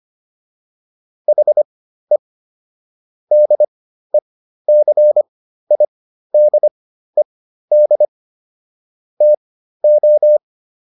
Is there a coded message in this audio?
Yes: Morse code from 1.3 to 10.4 s, heard as a keyed tone, about 25 wpm, spaced as for 13 wpm, 600 Hz. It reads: HE DECIDED TO